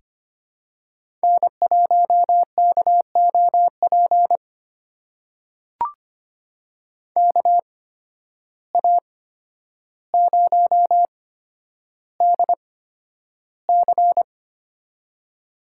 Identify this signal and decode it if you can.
Morse code, heard as a keyed tone, about 25 wpm, spaced as for 11 wpm, 700 Hz. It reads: N1KOP KA0DC
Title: USA Call Signs - 5 Characters - 8x Character Spacing - 25wpm